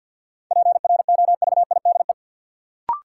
Morse code, heard as a keyed tone, about 52 wpm, 700 Hz. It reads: PROVIDE
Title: Sentences from Top 1000 Words - Review Part 2 - New Word Drill - 50wpm